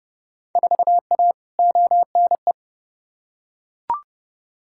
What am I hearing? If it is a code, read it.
Morse code, heard as a keyed tone, about 30 wpm, 700 Hz. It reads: 4A ONE
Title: Field Day Exchanges Encore - 30wpm